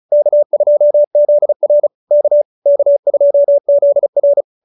Morse code, heard as a keyed tone, about 34 wpm, 590 Hz. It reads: K2ZR K K2ZR